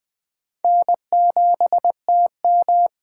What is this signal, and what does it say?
Morse code, heard as a keyed tone, about 20 wpm, 700 Hz. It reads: N7TM
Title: USA Call Signs - 4 Characters - 4x Character Spacing - 20wpm